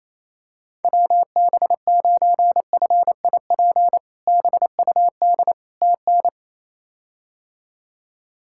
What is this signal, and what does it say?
Morse code, heard as a keyed tone, about 27 wpm, 700 Hz. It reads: WB9FIP BUD TN
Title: K1USN SST QSO Practice - 28wpm